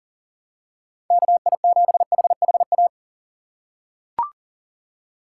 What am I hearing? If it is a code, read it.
Morse code, heard as a keyed tone, about 40 wpm, 700 Hz. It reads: KI7HHA
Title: USA Call Signs - 2x Character Spacing - 40wpm